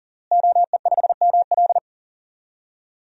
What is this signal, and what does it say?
Morse code, heard as a keyed tone, about 40 wpm, 700 Hz. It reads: OE5ML
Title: DXCC-100 Call Signs - 5 Characters - Repeated 3x - 40wpm